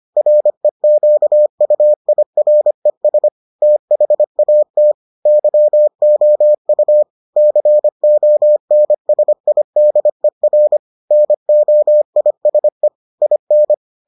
Morse code, heard as a keyed tone, about 25 wpm, 600 Hz. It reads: REQUIRES THAT YOU CONSIDER NOISE IN